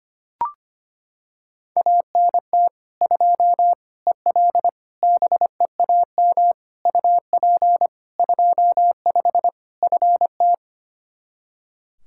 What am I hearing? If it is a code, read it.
Morse code, heard as a keyed tone, about 25 wpm, 700 Hz. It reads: ANT 2 EL BEAM UP 25 FT
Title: K1USN SST QSO Practice 25wpm